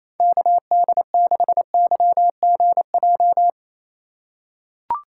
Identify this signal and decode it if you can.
Morse code, heard as a keyed tone, about 30 wpm, 700 Hz. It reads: KD6YGJ
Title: USA Call Signs - 3x Character Spacing - 28wpm